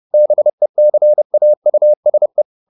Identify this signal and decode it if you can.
Morse code, heard as a keyed tone, about 30 wpm, 600 Hz. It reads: BECAUSE